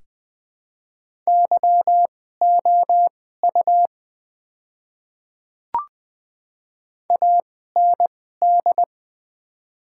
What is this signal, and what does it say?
Morse code, heard as a keyed tone, about 20 wpm, spaced as for 14 wpm, 700 Hz. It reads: YOU AND
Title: 2x Character Spacing - Top 200 Words - 20wpm